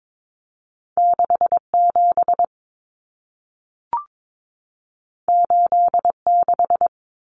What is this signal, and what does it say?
Morse code, heard as a keyed tone, about 22 wpm, 700 Hz. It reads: 67 86
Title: Numbers - 2-Digit 22wpm